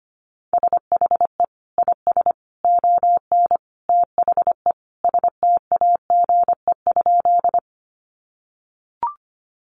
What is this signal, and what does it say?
Morse code, heard as a keyed tone, about 25 wpm, 700 Hz. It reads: SHE IS ON THE STAGE?
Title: Sentences from Top 1200 Words 25wpm